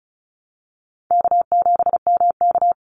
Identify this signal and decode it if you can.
Morse code, heard as a keyed tone, about 34 wpm, 700 Hz. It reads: K7MK
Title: NA SOTA Activator Callsigns 35wpm